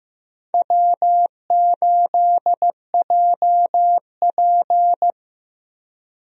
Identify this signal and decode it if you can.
Morse code, heard as a keyed tone, about 15 wpm, 700 Hz. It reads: W8JP
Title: USA Call Signs - 4 Characters - Repeated 3x - 15wpm